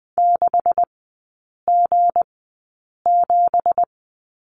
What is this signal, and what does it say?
Morse code, heard as a keyed tone, about 20 wpm, spaced as for 8 wpm, 700 Hz. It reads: 6G7
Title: 166 Lesson 27e - Number 7 - Three Characters 20wpm